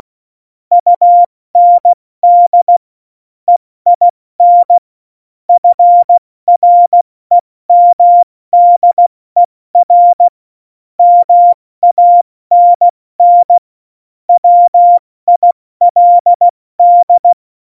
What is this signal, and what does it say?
Morse code, heard as a keyed tone, about 16 wpm, spaced as for 14 wpm, 700 Hz. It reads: UND EIN FREMDER MANN WILD
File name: Morsekurs / Witze / Witz 13. Oktober 2020, 70 F80